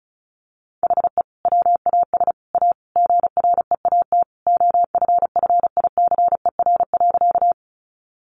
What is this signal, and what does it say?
Morse code, heard as a keyed tone, about 34 wpm, 700 Hz. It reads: HE WAS A GREAT OFFICER.
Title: Sentences from Top 1200 Words 35wpm